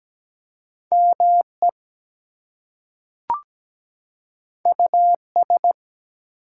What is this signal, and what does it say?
Morse code, heard as a keyed tone, about 17 wpm, 700 Hz. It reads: ME US
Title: Two Letter Words - 17wpm